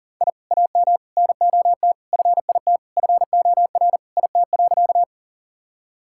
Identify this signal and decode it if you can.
Morse code, heard as a keyed tone, about 40 wpm, 700 Hz. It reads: I AM NOT FIT FOR IT.
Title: Sentences from Top 1000 Words - Review Part 5 - 40wpm